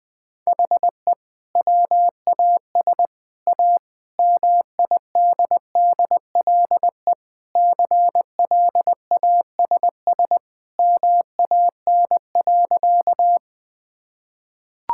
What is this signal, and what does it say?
Morse code, heard as a keyed tone, about 20 wpm, 700 Hz. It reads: HE WAS A MIDDLE CLASS MAN.
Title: Sentences from Top 1000 Words - Review Part 6 - 20wpm